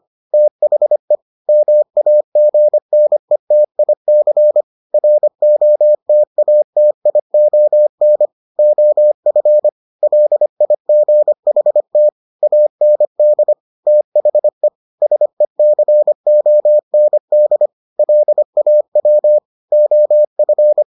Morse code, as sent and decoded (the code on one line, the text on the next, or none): - .... / -- .- --. -. - .. -.-. / .-. --- - .- - .. --- -. / --- ..-. / .-.. .. --. .... - / .- -. -.. / - .... / ... -.-. --- -. -.. / .-.. .- .-- / --- ..-.
THE MAGNETIC ROTATION OF LIGHT AND THE SECOND LAW OF